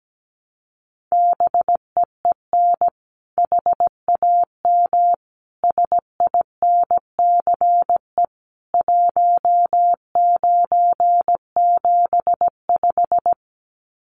Morse code, heard as a keyed tone, about 17 wpm, 700 Hz. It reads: BEEN HAM SINCE 1975
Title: Common QSO Phrases 17wpm